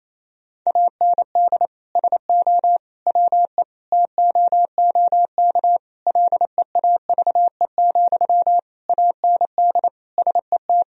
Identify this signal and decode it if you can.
Morse code, heard as a keyed tone, about 28 wpm, 700 Hz. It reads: AND SO WE TOOK LEAVE, AND SET
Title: Sentences from Top 1000 Words - Review Part 1 - 28wpm